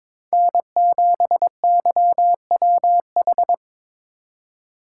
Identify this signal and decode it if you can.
Morse code, heard as a keyed tone, about 26 wpm, 700 Hz. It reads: N7YWH
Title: Speed Racing - USA 1x3 Call Signs 22wpm